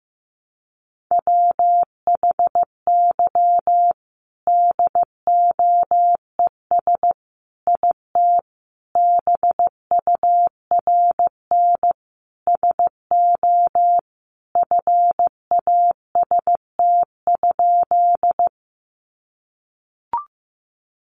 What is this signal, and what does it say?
Morse code, heard as a keyed tone, about 14 wpm, 700 Hz. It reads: WHY DOES IT BURN SO FAST?
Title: Sentences from Top 1300 Words 15wpm